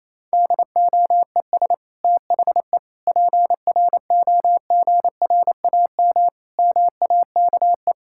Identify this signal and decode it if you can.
Morse code, heard as a keyed tone, about 28 wpm, 700 Hz. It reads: DOES THE PROGRAM MAKE
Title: Sentences from Top 1000 Words - Review Part 5 - 28wpm